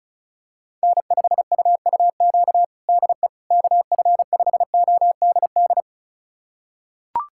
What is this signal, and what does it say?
Morse code, heard as a keyed tone, about 34 wpm, 700 Hz. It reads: N5UUQ DE KF5ODD
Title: USA Callsign DE Callsign 35wpm